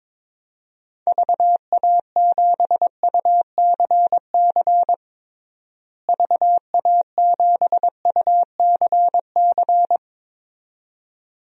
Morse code, heard as a keyed tone, about 22 wpm, 700 Hz. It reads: VA7UCC VA7UCC
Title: DXCC-100 Call Signs - 6 Characters - Repeated 2x - 22wpm